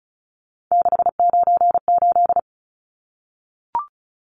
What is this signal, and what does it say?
Morse code, heard as a keyed tone, about 34 wpm, 700 Hz. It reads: 698
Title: Numbers - 3-Digit 35wpm